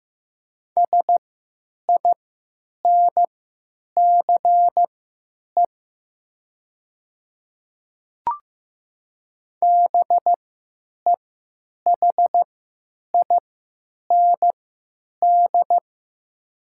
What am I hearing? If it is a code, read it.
Morse code, heard as a keyed tone, about 15 wpm, spaced as for 8 wpm, 700 Hz. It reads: SINCE BEHIND
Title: 3x Character Spacing - Top 500 Words - 15wpm